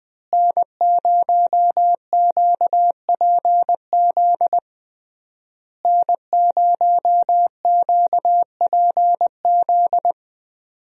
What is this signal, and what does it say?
Morse code, heard as a keyed tone, about 20 wpm, 700 Hz. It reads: N0QPZ N0QPZ
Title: USA Call Signs - 5 Characters - Repeated 3x - 20wpm